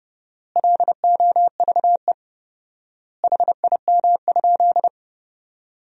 Morse code, heard as a keyed tone, about 30 wpm, 700 Hz. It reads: LOVE HIM?
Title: 4x Word-Spacing - Sentences from Top 400 Words - 30wpm